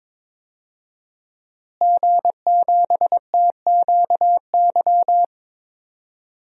Morse code, heard as a keyed tone, about 22 wpm, 700 Hz. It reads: G7TQY
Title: DXCC-100 Call Signs - 5 Characters - Repeated 3x - 22wpm